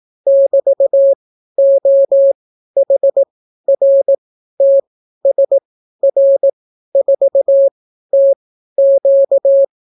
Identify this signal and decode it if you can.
Morse code, heard as a keyed tone, about 18 wpm, spaced as for 12 wpm, 550 Hz. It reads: <BT>OHRTSR4TQ